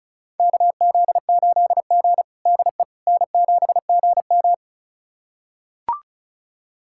Morse code, heard as a keyed tone, about 36 wpm, 700 Hz. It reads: KZ8G DE N7GM
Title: USA Callsign DE Callsign - 4 Characters - 35wpm